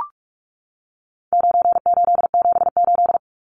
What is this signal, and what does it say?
Morse code, heard as a keyed tone, about 44 wpm, 700 Hz. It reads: 9878
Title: Numbers - 4-Digit 45wpm